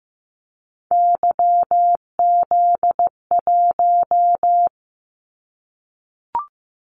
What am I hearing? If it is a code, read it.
Morse code, heard as a keyed tone, about 15 wpm, 700 Hz. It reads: YZ1
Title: Callsign Prefixes for Top 25 DXCC - 15wpm